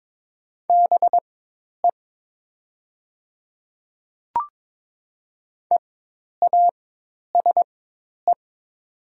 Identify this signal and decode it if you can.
Morse code, heard as a keyed tone, about 22 wpm, spaced as for 10 wpm, 700 Hz. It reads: BE EASE
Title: 4x Character Spacing - Top 300 Words - 22wpm